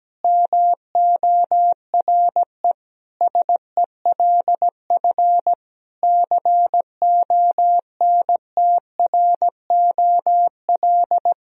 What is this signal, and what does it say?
Morse code, heard as a keyed tone, about 17 wpm, 700 Hz. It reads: MORE SELF CONTROL
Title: Sentences from Top 1000 Words - Review Part 3 - 17wpm